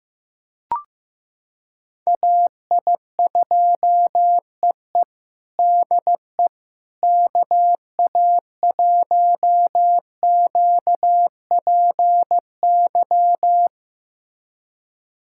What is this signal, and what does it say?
Morse code, heard as a keyed tone, about 15 wpm, 700 Hz. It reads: AI2EE DE KA1QPY
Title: USA Callsign DE Callsign 15wpm